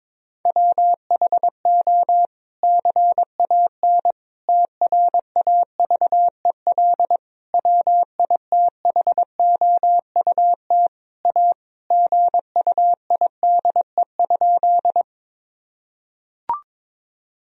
Morse code, heard as a keyed tone, about 22 wpm, 700 Hz. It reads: WHO CAN TRAVEL WITHOUT A GUIDE?
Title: Sentences from Top 1000 Words - Review Part 3 - 22wpm